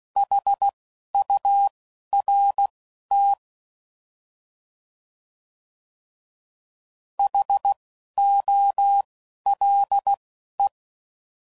Morse code, heard as a keyed tone, about 16 wpm, spaced as for 11 wpm, 800 Hz. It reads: HURT HOLE